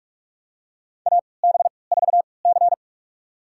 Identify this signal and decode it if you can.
Morse code, heard as a keyed tone, about 41 wpm, spaced as for 25 wpm, 700 Hz. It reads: AB4C
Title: USA Call Signs - 4 Characters - 3x Character Spacing - 45wpm